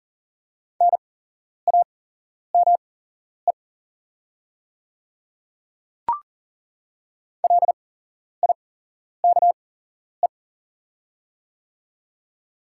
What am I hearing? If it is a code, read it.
Morse code, heard as a keyed tone, about 40 wpm, spaced as for 11 wpm, 700 Hz. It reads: NAME LIKE
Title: 8x Character Spacing - Top 300 Words - 40wpm